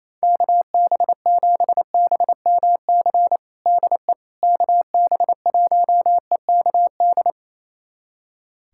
Morse code, heard as a keyed tone, about 27 wpm, 700 Hz. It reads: KB7BMC DE KB1EKD
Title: USA Callsign DE Callsign - 6 Characters - 28wpm